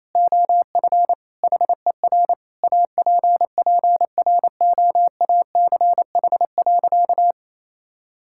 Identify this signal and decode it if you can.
Morse code, heard as a keyed tone, about 28 wpm, 700 Hz. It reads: OF HER APPROACH.
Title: Sentences from Top 1000 Words - Review Part 6 - 28wpm